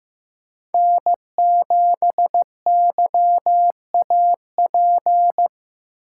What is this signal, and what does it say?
Morse code, heard as a keyed tone, about 15 wpm, 700 Hz. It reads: N7YAP
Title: USA Call Signs - 5 Characters - 3x Character Spacing - 15wpm